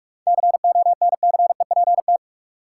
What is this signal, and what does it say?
Morse code, heard as a keyed tone, about 44 wpm, 700 Hz. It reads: CONCEPT
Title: Seven Letter Words - 45wpm